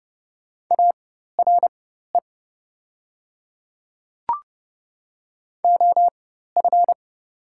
Morse code, heard as a keyed tone, about 30 wpm, spaced as for 14 wpm, 700 Hz. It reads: ARE OF
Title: 4x Character Spacing - Top 200 Words - 30wpm